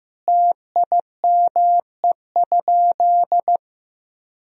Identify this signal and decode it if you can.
Morse code, heard as a keyed tone, about 15 wpm, 700 Hz. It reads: TIME?
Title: Sentences from Top 1000 Words - Review Part 1 - 15wpm